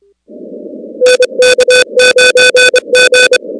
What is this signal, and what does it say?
Morse code, heard as a keyed tone, about 27 wpm, 500 Hz. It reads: NK9G
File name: contests / 2019-cw-wpx-kt5j-k5pi- -2330 0-19